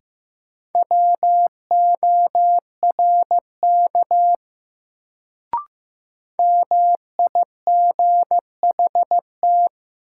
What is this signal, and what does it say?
Morse code, heard as a keyed tone, about 15 wpm, 700 Hz. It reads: WORK MIGHT
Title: Sets of 2 Words - New Word Drill - 15wpm